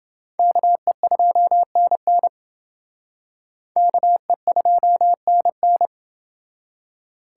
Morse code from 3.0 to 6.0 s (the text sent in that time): KE2NN